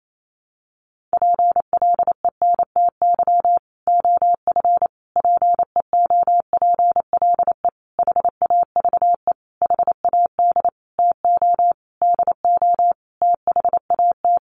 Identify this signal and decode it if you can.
Morse code, heard as a keyed tone, about 28 wpm, 700 Hz. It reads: PLENTY OF PEOPLE HAVE HAD TO DO THAT
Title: Sentences from Top 1300 Words 28wpm